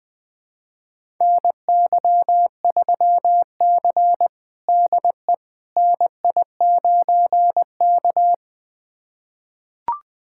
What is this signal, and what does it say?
Morse code, heard as a keyed tone, about 21 wpm, 700 Hz. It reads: NY3C DE NI9K
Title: USA Callsign DE Callsign - 4 Characters - 20wpm